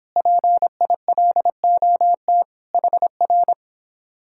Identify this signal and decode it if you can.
Morse code, heard as a keyed tone, about 26 wpm, 700 Hz. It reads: PILOT HR